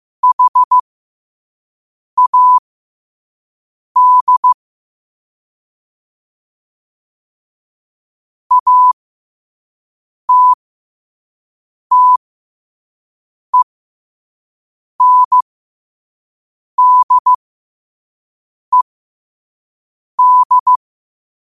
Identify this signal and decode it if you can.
Morse code, heard as a keyed tone, about 15 wpm, spaced as for 5 wpm, 1000 Hz. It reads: HAD ATTENDED
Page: Quotes for Sat, 16 Aug 2025 in Morse Code at 5 words per minute.